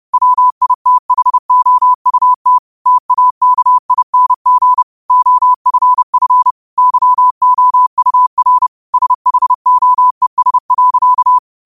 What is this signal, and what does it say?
Morse code, heard as a keyed tone, about 30 wpm, 1000 Hz. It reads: WITHOUT TAKING OFF YOUR SHOES.
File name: qotdfiles / 30wpm